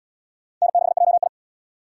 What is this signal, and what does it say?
Morse code, heard as a keyed tone, about 72 wpm, 700 Hz. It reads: N6JI